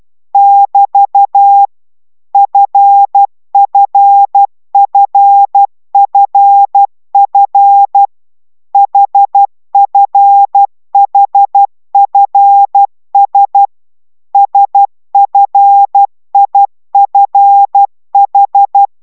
Morse code, heard as a keyed tone, about 12 wpm, 800 Hz. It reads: <BT> FFFFF HFHFS SFIFH
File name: taviro tanulas / cw 69 10 10